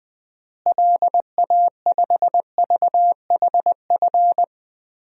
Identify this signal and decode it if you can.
Morse code, heard as a keyed tone, about 21 wpm, 700 Hz. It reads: LA5VHF
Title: DXCC-100 Call Signs - 6 Characters - Repeated 3x - 20wpm